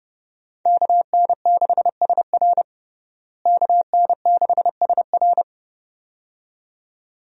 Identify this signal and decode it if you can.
Morse code, heard as a keyed tone, about 30 wpm, 700 Hz. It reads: KN6SR KN6SR
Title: USA Call Signs - 5 Characters - Repeated 2x - 30wpm